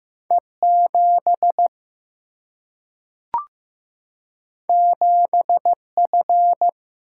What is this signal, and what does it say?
Morse code, heard as a keyed tone, about 15 wpm, 700 Hz. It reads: E7 7F
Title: Single-Letter Single-Number 15wpm